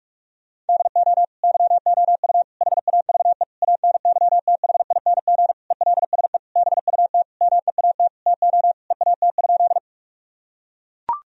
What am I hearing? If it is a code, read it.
Morse code, heard as a keyed tone, about 43 wpm, 700 Hz. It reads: DO YOU HAVE ANYTHING ELSE BUT MEAT TO EAT?
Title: Sentences from Top 1000 Words - Review Part 3 - 45wpm